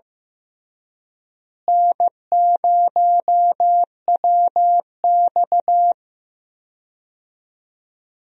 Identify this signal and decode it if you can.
Morse code, heard as a keyed tone, about 15 wpm, 700 Hz. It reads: N0WX
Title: USA Call Signs - 4 Characters - Repeated 3x - 15wpm